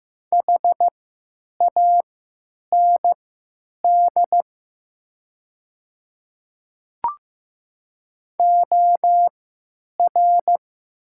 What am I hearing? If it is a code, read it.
Morse code, heard as a keyed tone, about 15 wpm, spaced as for 9 wpm, 700 Hz. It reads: HAND OR
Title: 3x Character Spacing - Top 200 Words - 15wpm